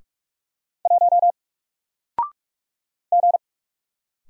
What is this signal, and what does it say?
Morse code, heard as a keyed tone, about 44 wpm, 700 Hz. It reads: 1 G
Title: Single Character (Letter:Number:Punctuation:Prosign) Alt 45wpm